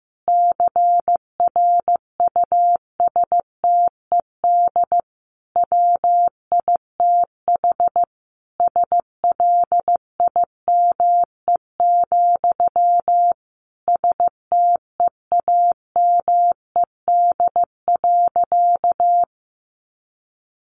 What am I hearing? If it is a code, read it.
Morse code, heard as a keyed tone, about 15 wpm, 700 Hz. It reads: CRUSTED WITH SLIME, STEAMED.
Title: A Sound Of Thunder 15wpm